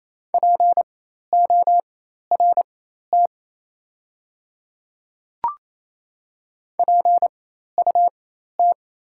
Morse code, heard as a keyed tone, about 28 wpm, spaced as for 13 wpm, 690 Hz. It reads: PORT PUT
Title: 4x Character Spacing - Top 200 Words - 28wpm